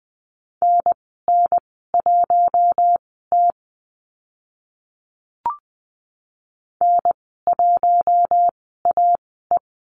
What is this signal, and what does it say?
Morse code, heard as a keyed tone, about 20 wpm, spaced as for 14 wpm, 700 Hz. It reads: NN1T N1AE
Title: 062 Lesson 11g - Letter h - Call Signs 20wpm